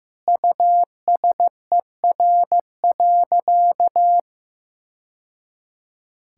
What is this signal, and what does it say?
Morse code, heard as a keyed tone, about 15 wpm, 700 Hz. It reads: USER.
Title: Sentences from Top 1000 Words - Review Part 5 - 15wpm